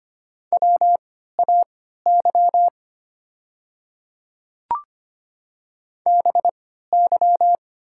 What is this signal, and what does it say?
Morse code, heard as a keyed tone, about 25 wpm, spaced as for 14 wpm, 690 Hz. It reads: WAY BY
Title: 3x Character Spacing - Top 100 Words - 25wpm